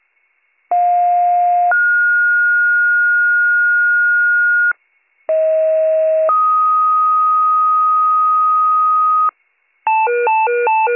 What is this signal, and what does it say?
Two-tone page, heard as a one-second tone, then a three-second tone, about 710 then 1430 Hz, about 640 then 1250 Hz, with noise. It is heard from 0.7 to 9.3 s.